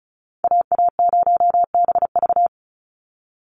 Morse code, heard as a keyed tone, about 34 wpm, 700 Hz. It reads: AA0BV
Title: NA SOTA Activator Callsigns 35wpm